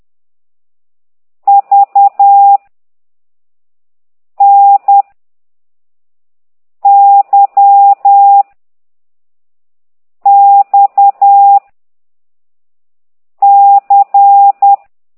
Morse code, heard as a keyed tone, about 10 wpm, 800 Hz. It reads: V N Y X C
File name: exerc7